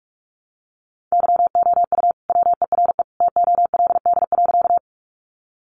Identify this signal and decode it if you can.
Morse code, heard as a keyed tone, about 44 wpm, 700 Hz. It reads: YOU WERE TOLD.
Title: Sentences from Top 400 Words 45wpm